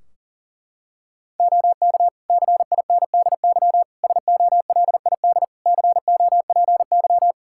Morse code, heard as a keyed tone, about 40 wpm, 700 Hz. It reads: OK CINDY SOLID COPY
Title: K1USN SST QSO Practice 40wpm